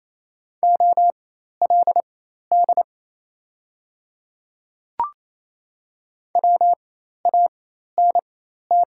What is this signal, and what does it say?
Morse code, heard as a keyed tone, about 28 wpm, spaced as for 13 wpm, 700 Hz. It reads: OLD WANT
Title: 4x Character Spacing - Top 500 Words - 28wpm